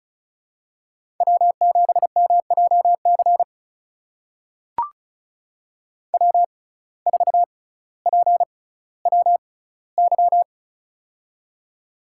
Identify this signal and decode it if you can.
Morse code, heard as a keyed tone, about 36 wpm, spaced as for 18 wpm, 700 Hz. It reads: W7MJC W4PWY
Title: USA Call Signs - 5 Characters - 6x Character Spacing - 35wpm